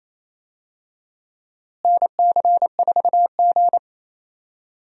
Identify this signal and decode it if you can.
Morse code, heard as a keyed tone, about 28 wpm, 700 Hz. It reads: NC4G
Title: USA Call Signs - 4 Characters - Repeated 2x - 28wpm